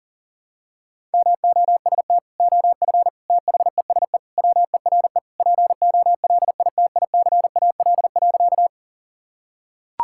Morse code, heard as a keyed tone, about 40 wpm, 700 Hz. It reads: MOST OF THESE WERE POLITICAL.
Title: Sentences from Top 1000 Words - Review Part 5 - 40wpm